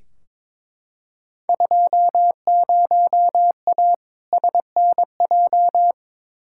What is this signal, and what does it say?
Morse code, heard as a keyed tone, about 22 wpm, 700 Hz. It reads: 20A SNJ
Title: Field Day Exchanges Encore - 22wpm